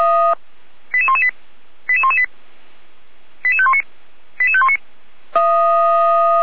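Five-tone selective call (ZVEI), heard as quick runs of five tones, tones of about 70 ms. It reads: 88118 88118 88519 88519
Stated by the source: Received by SDR, logged on a server of Feuerwehr Orlamünde (fire brigade).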